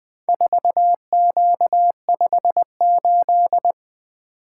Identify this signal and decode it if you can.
Morse code, heard as a keyed tone, about 20 wpm, 700 Hz. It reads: 4Q58
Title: Four Letter-Number Combinations - 20wpm